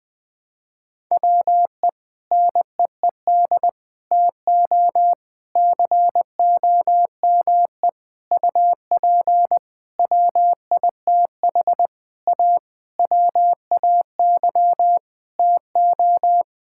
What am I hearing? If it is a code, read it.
Morse code, heard as a keyed tone, about 20 wpm, 700 Hz. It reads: WE NEED TO COME UP WITH A WAY TO
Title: Sentences with 7-Word Phrases - 20wpm